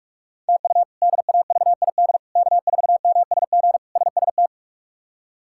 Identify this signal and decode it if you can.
Morse code, heard as a keyed tone, about 46 wpm, 700 Hz. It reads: TU DAVID K4MSG SST
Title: K1USN SST QSO Practice - 45wpm